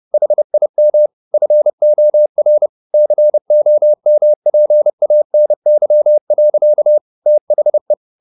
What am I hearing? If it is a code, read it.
Morse code, heard as a keyed tone, about 30 wpm, 600 Hz. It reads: HIM FOR COMPANY. THE
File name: the old man and the sea / the old man and the sea 1 30wpm